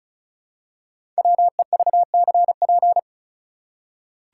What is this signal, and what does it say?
Morse code, heard as a keyed tone, about 36 wpm, 700 Hz. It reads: WEVCP
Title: Code Groups - 5 Letter:Numbers - 35wpm